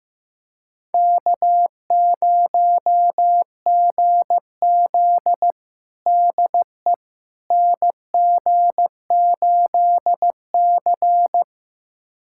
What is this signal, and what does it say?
Morse code, heard as a keyed tone, about 15 wpm, 700 Hz. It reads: K0GZ DE NG8C
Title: USA Callsign DE Callsign - 4 Characters - 15wpm